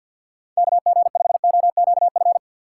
Morse code, heard as a keyed tone, about 49 wpm, 700 Hz. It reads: KG5OQF
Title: USA Call Signs - 6 Characters - 3x Character Spacing - 50wpm